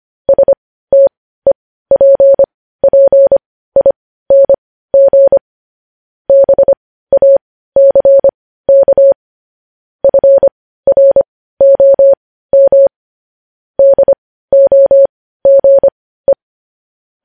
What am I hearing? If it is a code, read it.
Morse code, heard as a keyed tone, about 25 wpm, spaced as for 15 wpm, 550 Hz. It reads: STEPPING BACK FROM DOGE
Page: News Headlines in Morse Code at 15 WPM